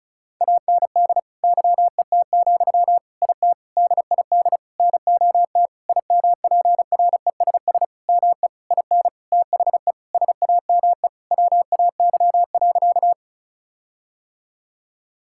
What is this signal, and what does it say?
Morse code, heard as a keyed tone, about 34 wpm, 700 Hz. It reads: AND YET, IT DID NOT IMPRESS ME IN THE SAME WAY.
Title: Sentences from Top 1600 Words 35wpm